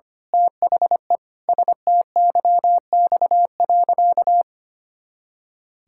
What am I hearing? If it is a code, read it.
Morse code, heard as a keyed tone, about 25 wpm, 700 Hz. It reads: THE STYX.
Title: Pangram Challenge v2 - 25wpm